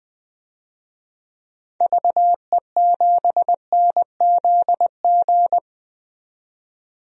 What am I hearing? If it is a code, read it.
Morse code, heard as a keyed tone, about 20 wpm, 700 Hz. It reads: VE7NZG
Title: DXCC-100 Call Signs - 6 Characters - 20wpm